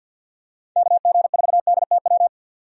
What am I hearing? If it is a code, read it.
Morse code, heard as a keyed tone, about 46 wpm, 700 Hz. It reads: KG4DTW
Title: USA Call Signs - 6 Characters - 50wpm